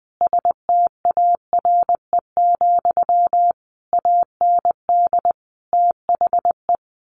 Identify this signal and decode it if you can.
Morse code, heard as a keyed tone, about 20 wpm, 700 Hz. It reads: STARE, AND THE
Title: A Sound Of Thunder 20wpm